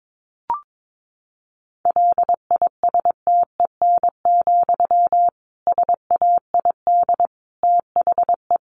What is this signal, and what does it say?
Morse code, heard as a keyed tone, about 22 wpm, 700 Hz. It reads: LISTEN, SAID THE
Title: Sentences from Top 600 Words 22wpm